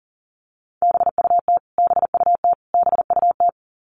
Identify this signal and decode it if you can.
Morse code, heard as a keyed tone, about 40 wpm, 700 Hz. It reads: BUT BUT BUT
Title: Top 500 Words Repeated 40wpm